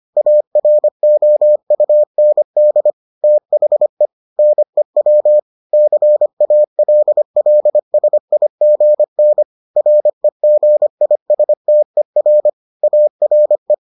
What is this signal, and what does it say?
Morse code, heard as a keyed tone, about 25 wpm, 600 Hz. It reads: AROUND THE NEW CALLSIGN REGISTER ARE